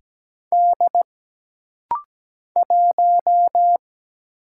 Morse code, heard as a keyed tone, about 17 wpm, 700 Hz. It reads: D 1
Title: Single Letter-Number Alt - Rapid-Fire 17wpm